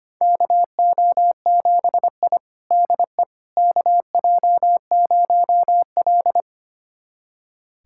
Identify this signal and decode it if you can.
Morse code, heard as a keyed tone, about 25 wpm, 700 Hz. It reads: KO7I DE KJ0L
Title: USA Callsign DE Callsign - 4 Characters - 25wpm